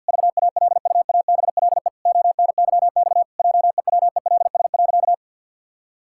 Morse code, heard as a keyed tone, about 51 wpm, 700 Hz. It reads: VALUABLE ONYX JEWELS.
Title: Pangram Challenge v2 - 50wpm